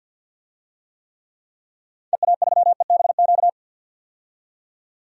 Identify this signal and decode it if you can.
Morse code, heard as a keyed tone, about 51 wpm, 700 Hz. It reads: EA3EBQ
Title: DXCC-100 Call Signs - 6 Characters - 50wpm